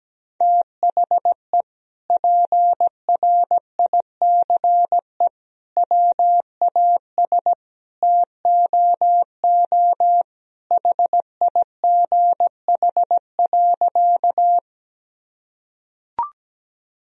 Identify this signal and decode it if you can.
Morse code, heard as a keyed tone, about 17 wpm, 700 Hz. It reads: THE PRICE WAS TOO HIGH.
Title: Sentences from Top 1000 Words - Review Part 4 - 17wpm